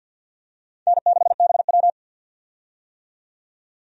Morse code, heard as a keyed tone, about 49 wpm, 700 Hz. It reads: N6BW
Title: USA Call Signs - 4 Characters - 50wpm